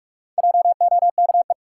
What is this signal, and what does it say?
Morse code, heard as a keyed tone, about 42 wpm, 700 Hz. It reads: JOKE
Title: Four Letter Words - 45wpm